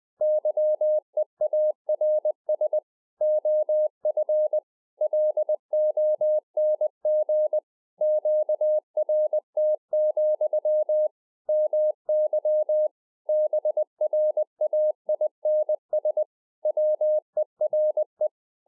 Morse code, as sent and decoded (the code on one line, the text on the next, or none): -.-- . .- .-. ... / --- ..-. / .-.. --- -. --. / --.- .-. - --..-- / -- -.-- / -... .-. .- .. -. ... / .-- . .-. .
YEARS OF LONG QRT, MY BRAINS WERE